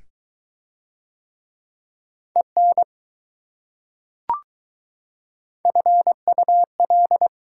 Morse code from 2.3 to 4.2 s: EN

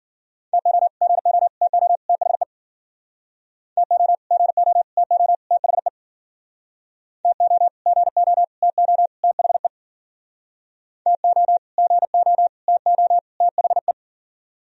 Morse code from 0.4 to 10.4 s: TO GO TO THE TO GO TO THE TO GO TO THE

